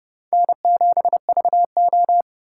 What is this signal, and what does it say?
Morse code, heard as a keyed tone, about 29 wpm, 700 Hz. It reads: N7VO